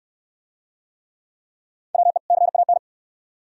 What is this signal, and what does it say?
Morse code, heard as a keyed tone, about 65 wpm, 700 Hz. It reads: WE CAN